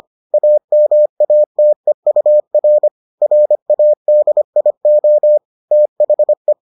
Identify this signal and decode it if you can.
Morse code, heard as a keyed tone, about 25 wpm, 600 Hz. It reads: AMATEUR RADIO THE